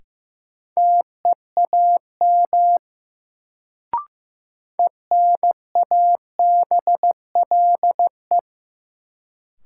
Morse code, heard as a keyed tone, about 16 wpm, 700 Hz. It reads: TEAM ENABLE
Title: Sentences from Top 1500 Words - New Word Drill - 15wpm